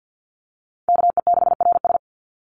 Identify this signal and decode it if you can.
Morse code, heard as a keyed tone, about 49 wpm, 700 Hz. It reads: KE6RS
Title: NA SOTA Chaser Callsigns 50wpm